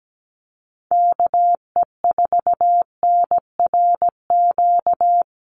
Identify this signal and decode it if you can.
Morse code, heard as a keyed tone, about 17 wpm, 700 Hz. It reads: KE4NRQ
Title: USA Call Signs Repeated 2x 17wpm